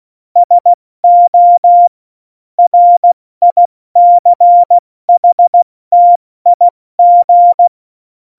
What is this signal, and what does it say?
Morse code, heard as a keyed tone, about 16 wpm, spaced as for 14 wpm, 700 Hz. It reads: SO RICHTIG
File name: Morsekurs / Witze / Witz 08. Dezember 2020, 70 F80